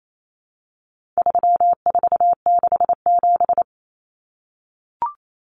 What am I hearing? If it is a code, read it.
Morse code, heard as a keyed tone, about 28 wpm, 700 Hz. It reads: 3467
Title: Numbers - 4-Digit 28wpm